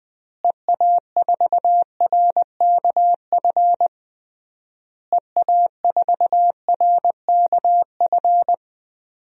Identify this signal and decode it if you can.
Morse code, heard as a keyed tone, about 20 wpm, 700 Hz. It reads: EA4RKF EA4RKF
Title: DXCC-100 Call Signs - 6 Characters - Repeated 3x - 20wpm